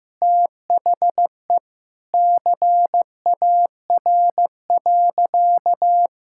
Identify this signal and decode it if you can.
Morse code, heard as a keyed tone, about 15 wpm, 700 Hz. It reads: THE CAR.
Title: Sentences from Top 1000 Words - Review Part 5 - 15wpm